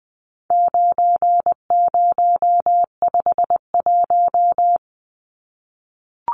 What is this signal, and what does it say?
Morse code, heard as a keyed tone, about 20 wpm, 700 Hz. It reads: 9051
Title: Numbers - 4-Digit 20wpm